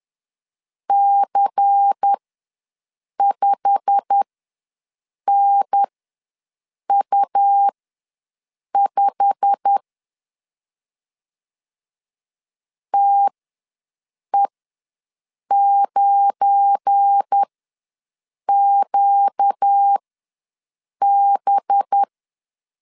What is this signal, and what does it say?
Morse code, heard as a keyed tone, about 10 wpm, spaced as for 6 wpm, 800 Hz. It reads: C5NU5 TE9QB
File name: LEZIONI / 8 YJ45 / 8 YJ45 ws 10wpm 3843